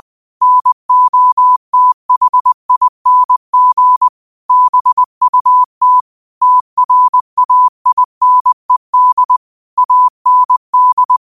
Morse code, heard as a keyed tone, about 20 wpm, 1000 Hz. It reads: NOTHING BUT TRAINED AND
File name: qotdfiles / 20wpm